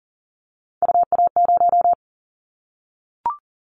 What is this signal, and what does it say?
Morse code, heard as a keyed tone, about 40 wpm, 700 Hz. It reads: UA0
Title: Callsign Prefixes for Top 25 DXCC - Repeated 3x - 40wpm